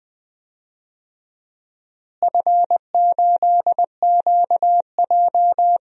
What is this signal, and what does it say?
Morse code, heard as a keyed tone, about 20 wpm, 700 Hz. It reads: F8QJ
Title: DXCC-100 Call Signs - 4 Characters - Repeated 3x - 20wpm